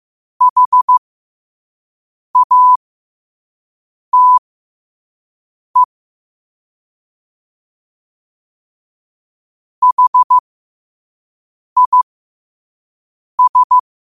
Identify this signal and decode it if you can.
Morse code, heard as a keyed tone, about 15 wpm, spaced as for 5 wpm, 1000 Hz. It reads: HATE HIS
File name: qotdfiles / 5wpm